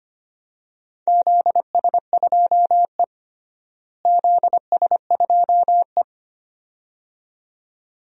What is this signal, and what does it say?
Morse code, heard as a keyed tone, about 25 wpm, 700 Hz. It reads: ZS2E ZS2E
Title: DXCC-100 Call Signs - 4 Characters - Repeated 2x - 25wpm